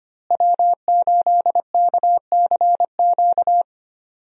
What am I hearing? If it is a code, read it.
Morse code, heard as a keyed tone, about 25 wpm, 700 Hz. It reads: W8KCQ
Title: USA Call Signs - 5 Characters - Repeated 2x - 25wpm